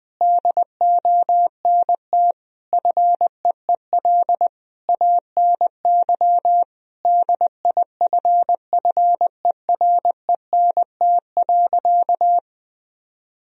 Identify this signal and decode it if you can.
Morse code, heard as a keyed tone, about 20 wpm, 700 Hz. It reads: DONT FEEL ANY DIFFERENT.
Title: Sentences from Top 1000 Words - Review Part 3 - 20wpm